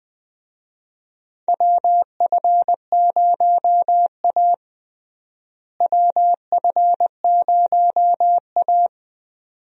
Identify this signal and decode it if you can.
Morse code, heard as a keyed tone, about 20 wpm, 700 Hz. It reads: WF0A WF0A
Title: USA Call Signs - 4 Characters - Repeated 2x - 20wpm